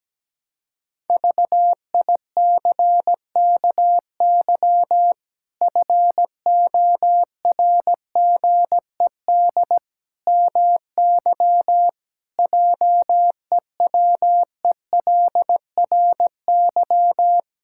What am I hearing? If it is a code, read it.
Morse code, heard as a keyed tone, about 17 wpm, 700 Hz. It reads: VICKY FORGED MY JEWELRY
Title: Pangram Challenge v2 - 17wpm